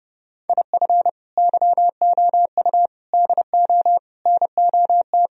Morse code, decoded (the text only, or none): IF YOU DO NOT